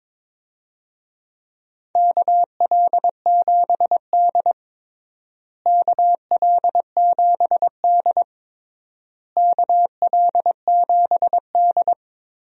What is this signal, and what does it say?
Morse code, heard as a keyed tone, about 22 wpm, 700 Hz. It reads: KL7D KL7D KL7D
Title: DXCC-100 Call Signs - 4 Characters - Repeated 3x - 22wpm